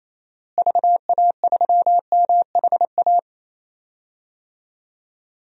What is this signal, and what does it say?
Morse code, heard as a keyed tone, about 28 wpm, 700 Hz. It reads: VA3MHA
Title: DXCC-100 Call Signs - 6 Characters - Repeated 2x - 28wpm